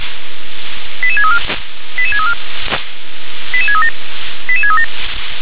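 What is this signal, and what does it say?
Five-tone selective call (ZVEI), heard as quick runs of five tones, tones of about 70 ms. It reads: 88634 88634 88637 88637
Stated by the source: ZVEI five-tone selective call received on an SDR.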